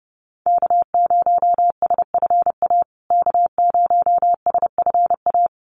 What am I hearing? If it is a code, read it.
Morse code, heard as a keyed tone, about 30 wpm, 700 Hz. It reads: K0SFA K0SFA
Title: USA Call Signs Repeated 2x 30wpm